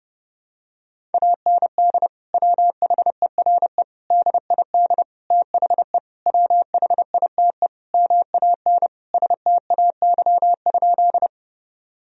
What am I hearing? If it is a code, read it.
Morse code, heard as a keyed tone, about 30 wpm, 700 Hz. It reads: AND WHERE DID THE WHITE MAN STAY?
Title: Sentences from Top 1000 Words - Review Part 1 - 30wpm